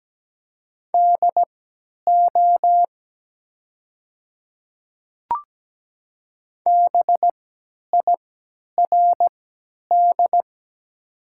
Morse code, heard as a keyed tone, about 17 wpm, spaced as for 10 wpm, 700 Hz. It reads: DO BIRD
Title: 3x Character Spacing - Top 500 Words - 17wpm